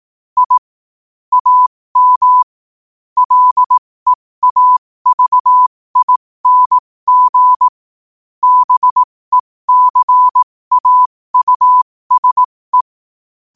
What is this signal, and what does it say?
Morse code, heard as a keyed tone, about 18 wpm, spaced as for 15 wpm, 1000 Hz. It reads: I AM LEAVING BECAUSE
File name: qotdfiles / 15wpm